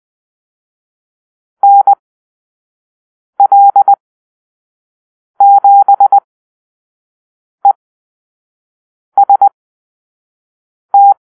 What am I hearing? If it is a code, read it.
Morse code, heard as a keyed tone, about 20 wpm, spaced as for 5 wpm, 800 Hz. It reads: NL7EST
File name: arrl / Call-Sign Drill-1